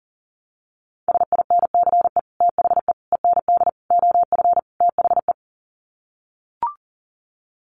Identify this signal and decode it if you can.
Morse code, heard as a keyed tone, about 40 wpm, 700 Hz. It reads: SINCE THE END OF THE
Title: Sets of 5 Words Encore 40wpm